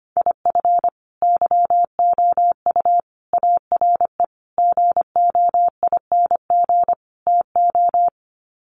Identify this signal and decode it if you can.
Morse code, heard as a keyed tone, about 25 wpm, 700 Hz. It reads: IF YOU ARE GOING TO
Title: Sets of 5 Words Encore2 25wpm